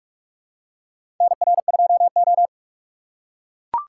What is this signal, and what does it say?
Morse code, heard as a keyed tone, about 41 wpm, 700 Hz. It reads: NR2O
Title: USA Call Signs - 4 Characters - 3x Character Spacing - 45wpm